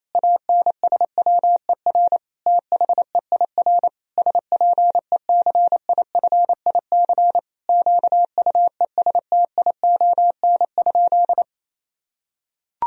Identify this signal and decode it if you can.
Morse code, heard as a keyed tone, about 28 wpm, 700 Hz. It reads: ANSWER THEIR SPECIFIC QUESTION?
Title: Sentences from Top 1000 Words - Review Part 3 - 28wpm